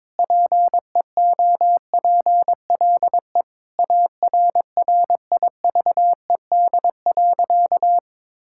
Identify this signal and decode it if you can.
Morse code, heard as a keyed tone, about 22 wpm, 700 Hz. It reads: PEOPLE ARRIVED.
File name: practice / 22wpm / Sentences with 3-Word Phrases - 22wpm